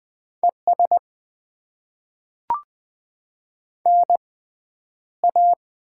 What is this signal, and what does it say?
Morse code, heard as a keyed tone, about 20 wpm, spaced as for 10 wpm, 700 Hz. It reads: ES NA
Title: Top 20 QSO Elements - 6x Character Spacing - 20wpm